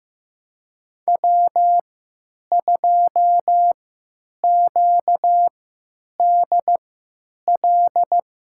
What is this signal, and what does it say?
Morse code, heard as a keyed tone, about 15 wpm, spaced as for 9 wpm, 700 Hz. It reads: W2QDL
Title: USA Call Signs - 5 Characters - 3x Character Spacing - 15wpm